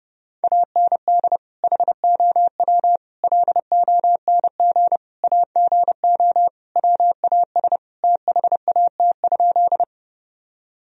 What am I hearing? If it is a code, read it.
Morse code, heard as a keyed tone, about 32 wpm, 700 Hz. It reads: AND HOW LONG AGO WAS THAT?
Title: Sentences from Top 1000 Words - Review Part 3 - 30wpm